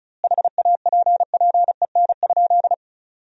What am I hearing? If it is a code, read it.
Morse code, heard as a keyed tone, about 36 wpm, 700 Hz. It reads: HAPPEN?
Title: 2x Word-Spacing - Sentences from Top 400 Words - 35wpm